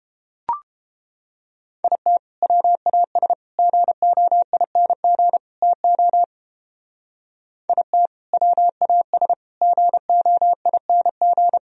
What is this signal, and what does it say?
Morse code, heard as a keyed tone, about 31 wpm, 700 Hz. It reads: IT WAS GOING TO IT WAS GOING